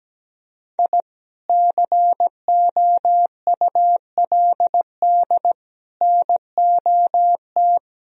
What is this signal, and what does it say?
Morse code, heard as a keyed tone, about 17 wpm, 700 Hz. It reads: I COULD NOT
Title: Sentences from Top 1000 Words - Review Part 1 - 17wpm